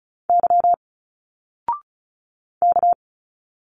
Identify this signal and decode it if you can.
Morse code, heard as a keyed tone, about 34 wpm, 700 Hz. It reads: Y K
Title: Single Letters 35wpm